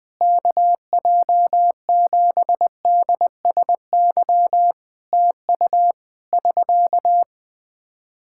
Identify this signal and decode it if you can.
Morse code, heard as a keyed tone, about 21 wpm, 700 Hz. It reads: KJ7DSY TU <SK>
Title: K1USN SST QSO Practice 20wpm